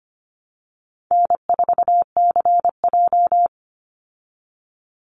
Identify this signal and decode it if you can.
Morse code, heard as a keyed tone, about 25 wpm, 700 Hz. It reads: N4CJ
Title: Easy Call Signs Repeated 3x - 25wpm